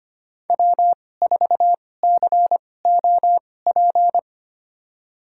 Morse code, heard as a keyed tone, about 25 wpm, spaced as for 18 wpm, 700 Hz. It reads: W4COP
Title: USA Call Signs - 5 Characters - 2x Character Spacing - 25wpm